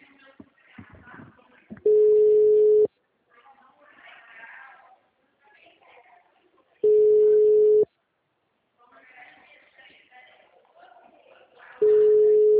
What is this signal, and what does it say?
Ringback tone (Europe, 425 Hz), with speech.
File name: out- -2019- - - .10089